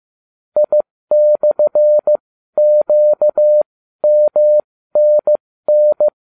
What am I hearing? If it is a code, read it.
Morse code, heard as a keyed tone, about 15 wpm, spaced as for 12 wpm, 600 Hz. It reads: I/QMNN